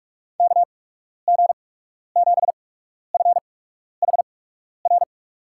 Morse code, heard as a keyed tone, about 46 wpm, spaced as for 12 wpm, 700 Hz. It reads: KG7FHR
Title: USA Call Signs - 8x Character Spacing - 45wpm